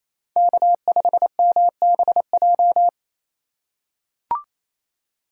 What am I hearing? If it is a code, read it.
Morse code, heard as a keyed tone, about 28 wpm, 700 Hz. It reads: K5MBJ
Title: USA Call Signs - 5 Characters - 2x Character Spacing - 28wpm